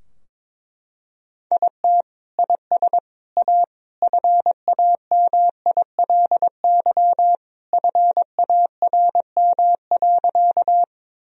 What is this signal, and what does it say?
Morse code, heard as a keyed tone, about 22 wpm, 700 Hz. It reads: IT IS A FAMILY FARM.
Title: Sentences from Top 1000 Words - Review Part 2 - 22wpm